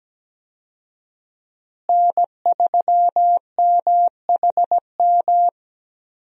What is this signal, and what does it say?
Morse code, heard as a keyed tone, about 17 wpm, 700 Hz. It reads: N3MHM
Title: USA Call Signs - 5 Characters - Repeated 3x - 17wpm